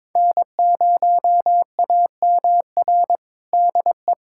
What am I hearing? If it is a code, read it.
Morse code, heard as a keyed tone, about 22 wpm, 700 Hz. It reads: N0AMR DE
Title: USA Callsign DE Callsign - 5 Characters - 22wpm